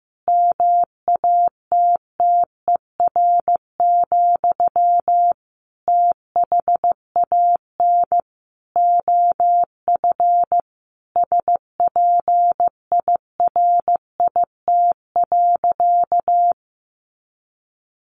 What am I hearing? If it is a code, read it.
Morse code, heard as a keyed tone, about 15 wpm, 700 Hz. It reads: MATTER, THAN OF SPIRIT.
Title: Sentences from Top 1400 Words 15wpm